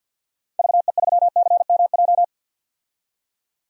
Morse code, heard as a keyed tone, about 52 wpm, 700 Hz. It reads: VE3CMJ
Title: DXCC-100 Call Signs - 6 Characters - 50wpm